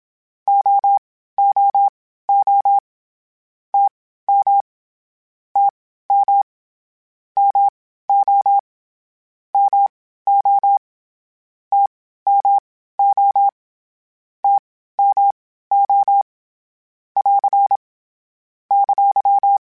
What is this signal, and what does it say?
Morse code, heard as a keyed tone, about 27 wpm, spaced as for 15 wpm, 800 Hz. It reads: OOO TM TM MO MO TMO TMO <AR> !
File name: Short Runs / Short Runs-mp3s / All DAHs @25